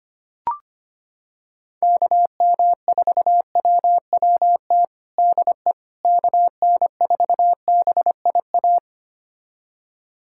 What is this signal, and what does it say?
Morse code, heard as a keyed tone, about 25 wpm, 700 Hz. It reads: KM4WWT DE KN4BIA
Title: USA Callsign DE Callsign 25wpm